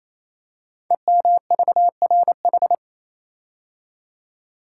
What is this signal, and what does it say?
Morse code, heard as a keyed tone, about 27 wpm, 700 Hz. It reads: EMVRH
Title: Code Groups - 5 Letters - 28wpm